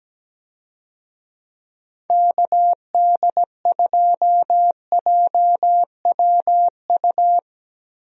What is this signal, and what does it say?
Morse code, heard as a keyed tone, about 17 wpm, 700 Hz. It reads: KD2JWU
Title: USA Call Signs - 6 Characters - Repeated 3x - 17wpm